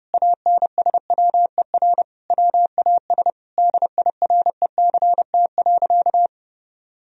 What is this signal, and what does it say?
Morse code, heard as a keyed tone, about 32 wpm, 700 Hz. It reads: ANSWER WAS DIRECT.
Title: Sentences from Top 1000 Words - Review Part 2 - 30wpm